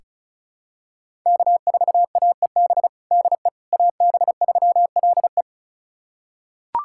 Morse code, heard as a keyed tone, about 36 wpm, 700 Hz. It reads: K4AEB DE AB3LE
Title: USA Callsign DE Callsign - 5 Characters - 35wpm